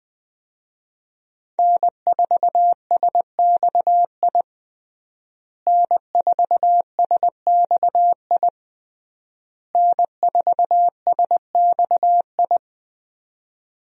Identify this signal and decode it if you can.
Morse code, heard as a keyed tone, about 20 wpm, 700 Hz. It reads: N4SXI N4SXI N4SXI
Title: USA Call Signs - 5 Characters - Repeated 3x - 20wpm